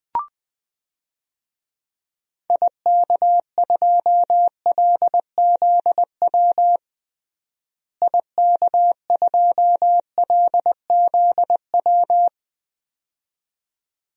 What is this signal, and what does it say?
Morse code, heard as a keyed tone, about 20 wpm, 700 Hz. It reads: IK2LZW IK2LZW
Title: DXCC-100 Call Signs - 6 Characters - Repeated 2x - 20wpm